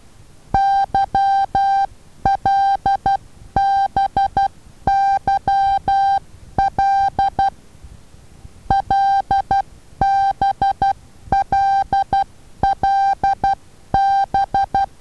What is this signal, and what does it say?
Morse code, heard as a keyed tone, about 12 wpm, 790 Hz. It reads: YLBYL LBLLB
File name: Taviro HA6VA B